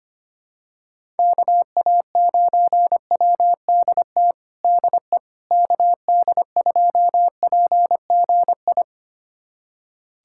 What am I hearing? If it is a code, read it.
Morse code, heard as a keyed tone, about 25 wpm, 700 Hz. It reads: KA9WDT DE KD2PGI
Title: USA Callsign DE Callsign - 6 Characters - 25wpm